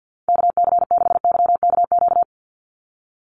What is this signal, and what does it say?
Morse code, heard as a keyed tone, about 50 wpm, 700 Hz. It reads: KC6YKQ
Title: USA Call Signs 50wpm